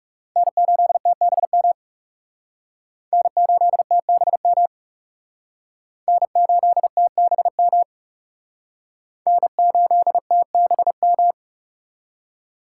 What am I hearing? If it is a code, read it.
Morse code, heard as a keyed tone, about 37 wpm, 700 Hz. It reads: N8TBM N8TBM N8TBM N8TBM
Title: Speed Racing - USA 1x3 Call Signs 30wpm